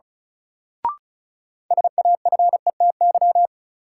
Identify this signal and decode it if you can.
Morse code, heard as a keyed tone, about 36 wpm, 700 Hz. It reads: SAFETY
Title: Sentences from Top 700 Words - New Word Drill - 35wpm